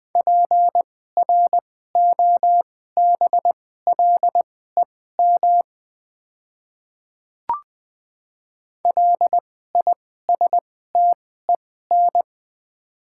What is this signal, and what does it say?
Morse code, heard as a keyed tone, about 20 wpm, spaced as for 15 wpm, 700 Hz. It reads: PROBLEM LISTEN
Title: 2x Character Spacing - Top 500 Words - 20wpm